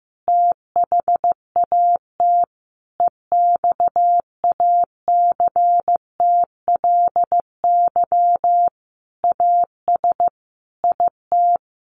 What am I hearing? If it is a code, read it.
Morse code, heard as a keyed tone, about 15 wpm, 700 Hz. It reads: THAT EXACTLY AS IT
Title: Sentences from Top 900 Words 15wpm